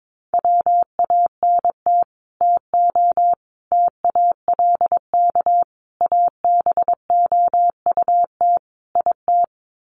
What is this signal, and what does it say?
Morse code, heard as a keyed tone, about 22 wpm, 700 Hz. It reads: WANT TO TALK ABOUT IT
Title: Sets of 5 Words Encore 22wpm